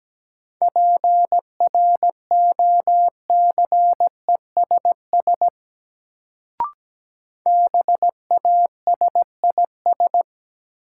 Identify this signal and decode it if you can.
Morse code, heard as a keyed tone, about 17 wpm, 700 Hz. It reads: PROCESS BASIS
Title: Sets of 3 Words - New Word Drill - 17wpm